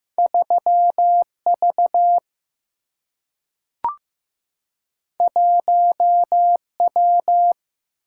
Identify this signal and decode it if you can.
Morse code, heard as a keyed tone, about 15 wpm, 700 Hz. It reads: 3V 1W
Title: Single-Letter Single-Number 15wpm